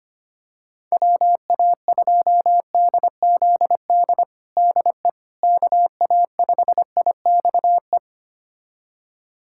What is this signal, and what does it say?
Morse code, heard as a keyed tone, about 25 wpm, 700 Hz. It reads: WA2DZD DE KA5IXE
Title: USA Callsign DE Callsign - 6 Characters - 25wpm